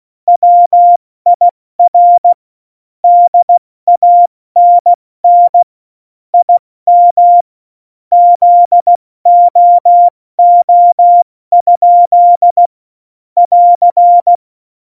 Morse code, heard as a keyed tone, about 16 wpm, 700 Hz. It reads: WIR DANN IM ZOO? <AR>
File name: Morsekurs / Witze / Witz 06. November 2018, 70 F80